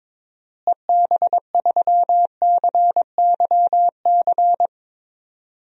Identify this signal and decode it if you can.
Morse code, heard as a keyed tone, about 22 wpm, 700 Hz. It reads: EB3CYC
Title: DXCC-100 Call Signs - 6 Characters - Repeated 3x - 22wpm